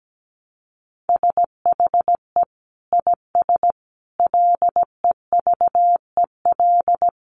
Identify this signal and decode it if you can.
Morse code, heard as a keyed tone, about 17 wpm, 700 Hz. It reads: SHE IS LEVEL
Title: Sentences from Top 700 Words 17wpm